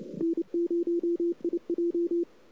Morse code, heard as a keyed tone, about 28 wpm, 340 Hz. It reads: N0IJ
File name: CW / 2008-ss-cw-k5tr-op-n5rz -05:00